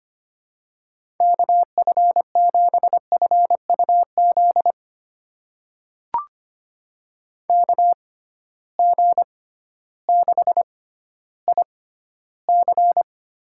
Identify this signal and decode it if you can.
Morse code, heard as a keyed tone, about 25 wpm, spaced as for 13 wpm, 700 Hz. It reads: KF7FUZ KG6IC
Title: USA Call Signs - 6 Characters - 6x Character Spacing - 25wpm